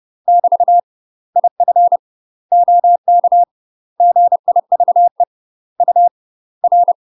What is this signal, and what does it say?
Morse code, heard as a keyed tone, about 30 wpm, spaced as for 24 wpm, 700 Hz. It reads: <BT> IF OK GIVE UR
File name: UK MASTER QSOs TRAINING / 30WPM / UK MASTER QSOs 0038